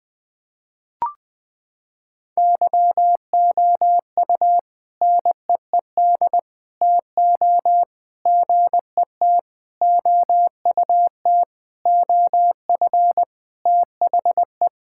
Morse code, heard as a keyed tone, about 20 wpm, 700 Hz. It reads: YOU NEED TO GET OUT OF THE
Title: Sentences with 4-Word Phrases - 20wpm